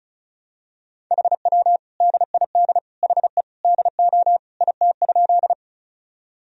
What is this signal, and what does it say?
Morse code, heard as a keyed tone, about 34 wpm, 700 Hz. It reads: HW DID HE DO IT?
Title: Sentences from Top 100 Words and Abbreviations 35wpm